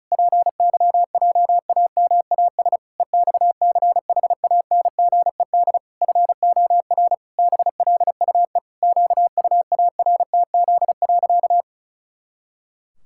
Morse code, heard as a keyed tone, about 36 wpm, 700 Hz. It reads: PYJAMAS EXCHANGED FOR BLUE QUARTZ.
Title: Pangram Challenge v2 - 35wpm